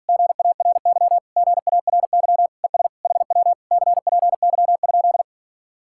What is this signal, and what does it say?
Morse code, heard as a keyed tone, about 52 wpm, 700 Hz. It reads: GARY GARY ES HW CPY?